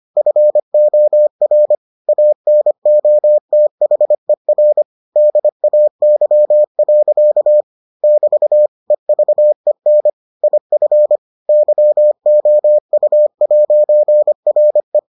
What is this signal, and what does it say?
Morse code, heard as a keyed tone, about 25 wpm, 600 Hz. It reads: FOR ANOTHER DAY. <BT> EVEN IF YOU'RE